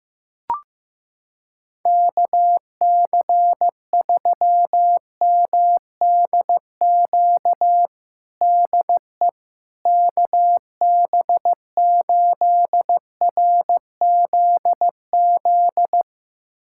Morse code, heard as a keyed tone, about 16 wpm, 700 Hz. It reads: KC3MDQ DE KB8RZZ
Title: USA Callsign DE Callsign 15wpm